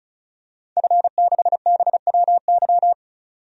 Morse code, heard as a keyed tone, about 36 wpm, 700 Hz. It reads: F6BWY
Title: DXCC-100 Call Signs - 5 Characters - 35wpm